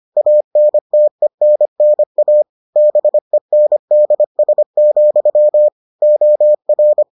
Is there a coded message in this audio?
Yes: Morse code from 0.2 to 7.0 s, heard as a keyed tone, about 25 wpm, 600 Hz. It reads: ANTENNA BENDS, OR